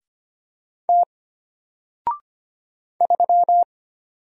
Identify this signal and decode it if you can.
Morse code, heard as a keyed tone, about 25 wpm, 700 Hz. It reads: T 3
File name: Single-Letter-Number-25wpm